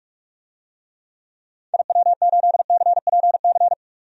Morse code, heard as a keyed tone, about 46 wpm, 700 Hz. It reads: IW8CPC
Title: DXCC-100 Call Signs - 6 Characters - 45wpm